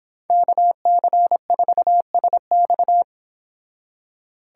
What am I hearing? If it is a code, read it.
Morse code, heard as a keyed tone, about 26 wpm, 700 Hz. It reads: KC4SX